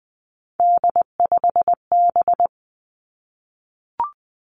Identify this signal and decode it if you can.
Morse code, heard as a keyed tone, about 20 wpm, 700 Hz. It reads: D5B
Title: Three Challenging Letter-Number Combinations 20wpm